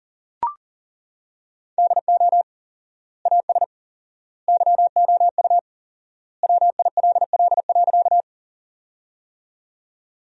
Morse code, heard as a keyed tone, about 40 wpm, 700 Hz. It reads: DO AS YOU WILL.
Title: 4x Word-Spacing - Sentences from Top 100 Words - 40wpm